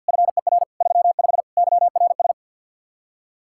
Morse code, heard as a keyed tone, about 62 wpm, 700 Hz. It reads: FER 35 YRS
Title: Speed-Racing Common QSO Phrases - 50wpm